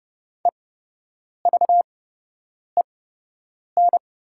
Morse code, heard as a keyed tone, about 30 wpm, spaced as for 8 wpm, 700 Hz. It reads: EVEN
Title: 8x Character Spacing - Top 200 Words - 30wpm